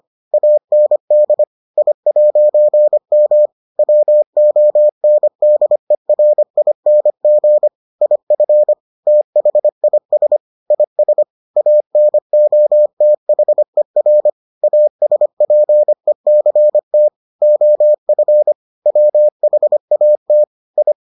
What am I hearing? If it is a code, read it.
Morse code, heard as a keyed tone, about 25 wpm, 600 Hz. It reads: AND I'M WONDERING IF THIS IS ANOTHER ASPECT OF WHAT I